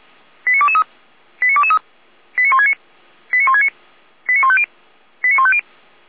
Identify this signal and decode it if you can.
Five-tone selective call (ZVEI), heard as quick runs of five tones, tones of about 80 ms. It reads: 89223 89223 89169 89169 89150 89150